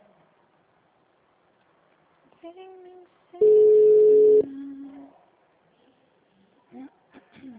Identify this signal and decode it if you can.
Ringback tone (Europe, 425 Hz), with speech.
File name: out- -2003- - - .24865